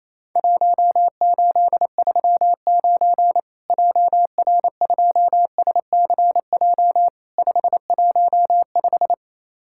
Morse code, heard as a keyed tone, about 28 wpm, 700 Hz. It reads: 1839 JR2SCJ 515
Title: Worked All Europe DX Contest - QTC Exchange - 28wpm